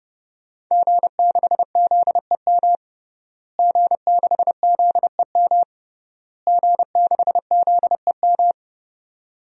Morse code, heard as a keyed tone, about 32 wpm, 700 Hz. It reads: G6ZEM G6ZEM G6ZEM
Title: DXCC-100 Call Signs - 5 Characters - Repeated 3x - 30wpm